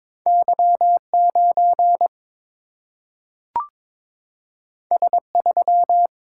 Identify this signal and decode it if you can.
Morse code, heard as a keyed tone, about 22 wpm, 700 Hz. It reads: Y 9 S 3
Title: Single-Letter Single-Number 22wpm